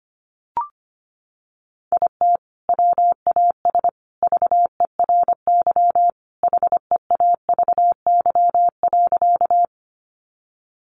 Morse code, heard as a keyed tone, about 25 wpm, 700 Hz. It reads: IT WAS VERY HEAVY.
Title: Sentences from Top 1100 Words 25wpm